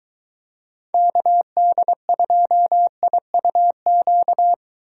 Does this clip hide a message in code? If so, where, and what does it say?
Yes: Morse code from 0.9 to 4.5 s, heard as a keyed tone, about 21 wpm, 700 Hz. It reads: KD2IUQ